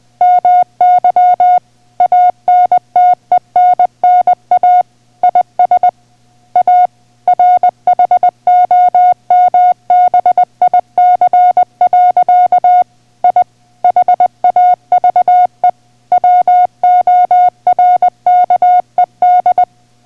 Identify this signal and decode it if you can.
Morse code, heard as a keyed tone, about 20 wpm, 720 Hz. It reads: MY ANTENNA IS A RHOMBIC. I HAVE WORKED